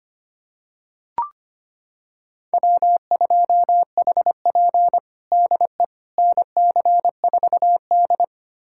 Morse code, heard as a keyed tone, about 25 wpm, 700 Hz. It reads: W2HP DE NC4D
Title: USA Callsign DE Callsign - 4 Characters - 25wpm